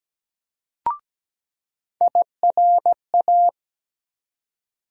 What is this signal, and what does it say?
Morse code, heard as a keyed tone, about 17 wpm, 700 Hz. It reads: IRA
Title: USA Call Sign Postfixes - 3 Letters 17wpm